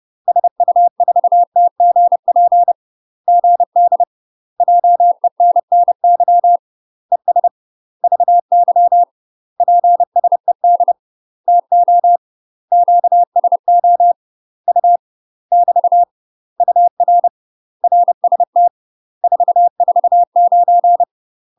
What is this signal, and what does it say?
Morse code, heard as a keyed tone, about 30 wpm, 700 Hz. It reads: SU4TGP GD JENNY ES VY PSED TO QSO U <BT> UR RST 449
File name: UK MASTER QSOs TRAINING / 30WPM / UK MASTER QSOs 0007